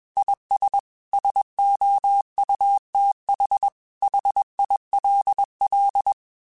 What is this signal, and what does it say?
Morse code, heard as a keyed tone, about 21 wpm, 800 Hz. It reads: IS SOUTH HILL
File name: QSOs / QSOs-mp3s / WA7QOX 2@20